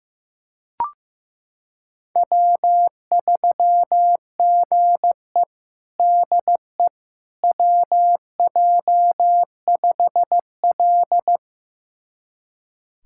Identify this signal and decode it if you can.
Morse code, heard as a keyed tone, about 15 wpm, 700 Hz. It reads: W3GE DE WJ5L
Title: USA Callsign DE Callsign - 4 Characters - 15wpm